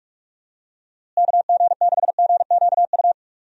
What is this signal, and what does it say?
Morse code, heard as a keyed tone, about 41 wpm, 700 Hz. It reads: KG6GQU
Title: USA Call Signs - 6 Characters - Repeated 3x - 45wpm